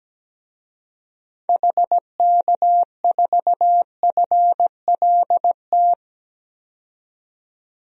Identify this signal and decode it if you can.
Morse code, heard as a keyed tone, about 17 wpm, 700 Hz. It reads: HK4FLT
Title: DXCC-100 Call Signs - 6 Characters - 17wpm